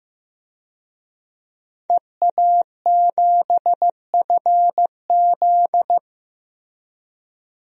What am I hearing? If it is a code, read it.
Morse code, heard as a keyed tone, about 15 wpm, 700 Hz. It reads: EA7FZ
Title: DXCC-100 Call Signs - 5 Characters - 15wpm